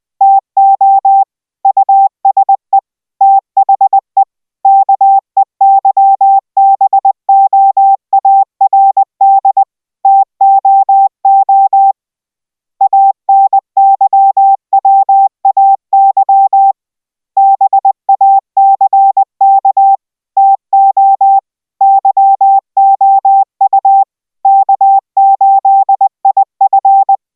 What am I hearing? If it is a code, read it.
Morse code, heard as a keyed tone, about 20 wpm, 780 Hz. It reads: TO USE THE KEYBOARD TOO ANYWAY BACK TO YOU K8IF